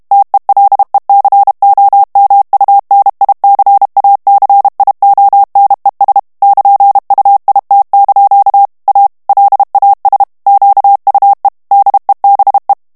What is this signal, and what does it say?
Morse code, heard as a keyed tone, about 32 wpm, 800 Hz. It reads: TELECOMUNICACIONES <KN>UIT) A LAS QUE DEBE
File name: morse mp3 / ESPAGNOL TEXTE EN CLAIR 1 VITESSE 30 MOTS MINUTES